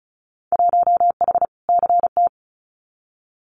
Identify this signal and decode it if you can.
Morse code, heard as a keyed tone, about 36 wpm, 700 Hz. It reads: 1H CT
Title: Winter Field Day Exchanges Encore 35wpm